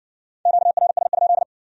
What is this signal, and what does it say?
Morse code, heard as a keyed tone, about 67 wpm, 700 Hz. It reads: QRS?